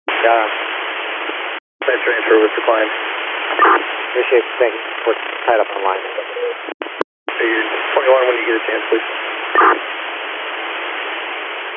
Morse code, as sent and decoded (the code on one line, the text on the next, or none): none